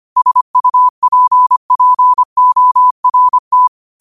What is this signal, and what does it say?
Morse code, heard as a keyed tone, about 25 wpm, 1000 Hz. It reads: SUPPORT
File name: qotdfiles / 25wpm